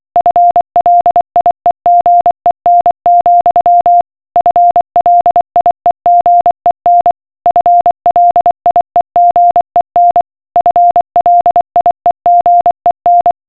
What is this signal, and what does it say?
Morse code, heard as a keyed tone, about 24 wpm, 700 Hz. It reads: FLIEGEN, FLIEGEN FLIEGEN FLIEGEN